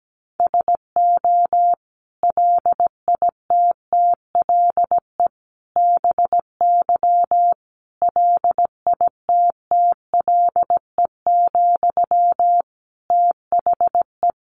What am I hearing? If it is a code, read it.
Morse code, heard as a keyed tone, about 17 wpm, 700 Hz. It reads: SO LITTLE BY LITTLE, THE
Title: Sentences from Top 1500 Words 17wpm